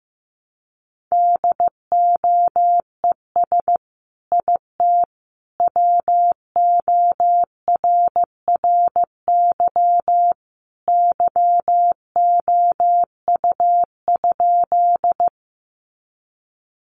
Morse code, heard as a keyed tone, about 15 wpm, 700 Hz. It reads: DOES IT WORRY YOU?
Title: Sentences from Top 1100 Words 15wpm